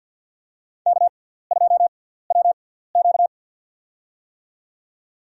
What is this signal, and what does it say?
Morse code, heard as a keyed tone, about 51 wpm, spaced as for 17 wpm, 700 Hz. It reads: K2WQ
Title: USA Call Signs - 4 Characters - 6x Character Spacing - 50wpm